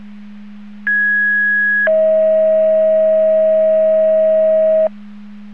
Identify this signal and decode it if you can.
Two-tone page, heard as a one-second tone, then a three-second tone, about 1650 then 640 Hz. It is heard from 0.9 to 4.9 s.